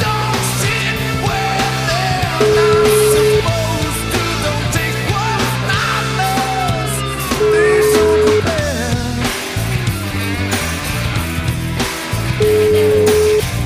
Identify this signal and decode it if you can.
Ringback tone (Europe, 425 Hz), with noise.